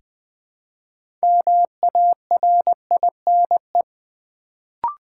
Morse code, heard as a keyed tone, about 20 wpm, 700 Hz. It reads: MARINE
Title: Build A Word - 2 Letters At A Time - 6-Letter Words - 20wpm